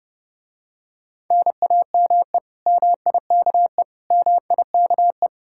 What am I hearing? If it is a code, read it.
Morse code, heard as a keyed tone, about 32 wpm, 700 Hz. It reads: NAME MIKE MIKE